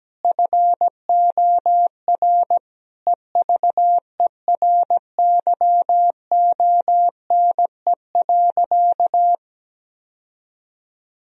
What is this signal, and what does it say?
Morse code, heard as a keyed tone, about 17 wpm, 700 Hz. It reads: FOR EVERYONE.
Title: Sentences from Top 1000 Words - Review Part 1 - 17wpm